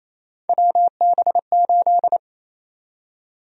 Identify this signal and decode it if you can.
Morse code, heard as a keyed tone, about 28 wpm, 700 Hz. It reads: WB8
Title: USA Call Sign Prefixes - 28wpm